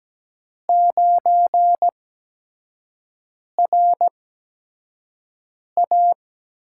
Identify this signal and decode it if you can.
Morse code, heard as a keyed tone, about 17 wpm, spaced as for 5 wpm, 700 Hz. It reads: 9RA
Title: USA Call Signs - 4 Characters - 8x Character Spacing - 17wpm